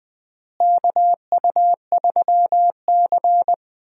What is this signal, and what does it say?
Morse code, heard as a keyed tone, about 20 wpm, 700 Hz. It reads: KU3C
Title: USA Call Signs - 4 Characters - Repeated 3x - 20wpm